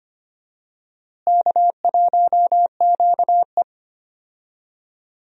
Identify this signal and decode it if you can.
Morse code, heard as a keyed tone, about 25 wpm, 700 Hz. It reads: K1QE
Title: USA Call Signs - 4 Characters - Repeated 2x - 25wpm